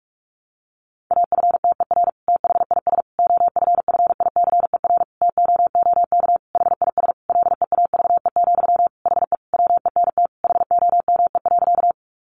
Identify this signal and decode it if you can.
Morse code, heard as a keyed tone, about 44 wpm, 700 Hz. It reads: AFTER THIS OFFICER TOOK HIS LEAVE, HE WENT HOME.
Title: Sentences from Top 1200 Words 45wpm